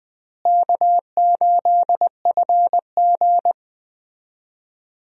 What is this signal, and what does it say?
Morse code, heard as a keyed tone, about 20 wpm, 700 Hz. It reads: K8FG
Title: USA Call Signs - 4 Characters - Repeated 3x - 20wpm